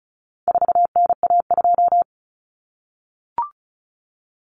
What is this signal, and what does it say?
Morse code, heard as a keyed tone, about 34 wpm, 700 Hz. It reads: 4NA2
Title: Common Cut Numbers - 4-Digit 35wpm